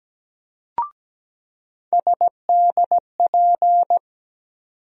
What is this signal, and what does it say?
Morse code, heard as a keyed tone, about 17 wpm, 700 Hz. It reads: SDP
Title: USA Call Sign Postfixes - 3 Letters 17wpm